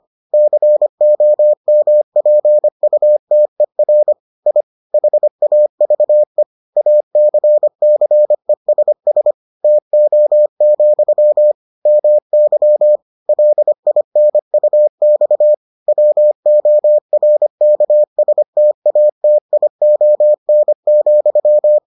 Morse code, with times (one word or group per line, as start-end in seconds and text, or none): 0.3-4.1 COMPUTER
4.5-4.6 I
4.9-6.4 HAVE
6.8-9.3 ACCESS
9.6-11.5 TO,
11.9-13.0 MY
13.3-15.5 LINUX
15.9-21.8 WORKSTATION,